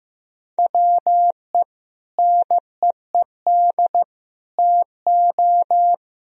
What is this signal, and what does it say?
Morse code, heard as a keyed tone, about 15 wpm, 700 Hz. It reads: WE NEED TO
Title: Sentences from Top 1000 Words - Review Part 4 - 15wpm